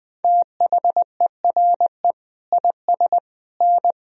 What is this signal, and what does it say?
Morse code, heard as a keyed tone, about 20 wpm, 700 Hz. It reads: THERE IS N
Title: Sentences from Top 500 Words and Abbreviations 20wpm